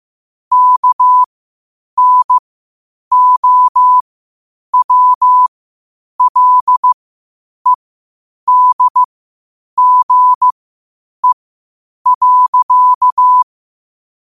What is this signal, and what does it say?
Morse code, heard as a keyed tone, about 15 wpm, spaced as for 8 wpm, 1000 Hz. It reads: KNOWLEDGE.
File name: qotdfiles / 8wpm